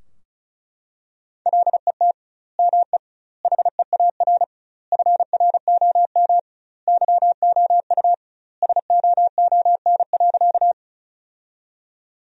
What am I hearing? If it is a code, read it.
Morse code, heard as a keyed tone, about 34 wpm, 700 Hz. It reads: LET ME HEAR FROM YOU SOON.
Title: 2x Word-Spacing - Sentences from Top 400 Words - 35wpm